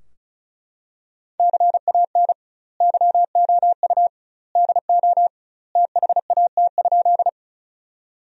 Morse code, heard as a keyed tone, about 36 wpm, 700 Hz. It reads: CAN YOU DO THAT?
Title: 2x Word-Spacing - Sentences from Top 100 Words - 35wpm